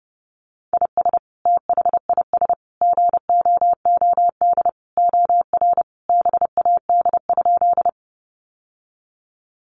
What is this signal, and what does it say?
Morse code, heard as a keyed tone, about 30 wpm, 700 Hz. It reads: IS THIS GOOD OR BAD?
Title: Sentences from Top 700 Words 30wpm